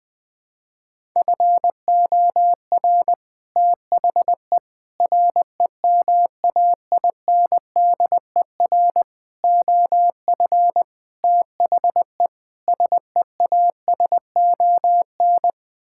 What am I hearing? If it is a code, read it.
Morse code, heard as a keyed tone, about 21 wpm, 700 Hz. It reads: FOR THE REMAINDER OF THE SEASON
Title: Sets of 6 Words - 20wpm